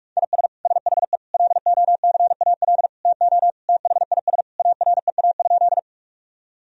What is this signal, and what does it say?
Morse code, heard as a keyed tone, about 41 wpm, 700 Hz. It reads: IS SHE LOCAL TO THIS AREA?
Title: Sentences from Top 1000 Words - Review Part 3 - 45wpm